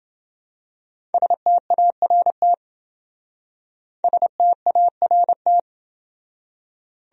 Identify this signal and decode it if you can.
Morse code, heard as a keyed tone, about 27 wpm, 700 Hz. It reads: START START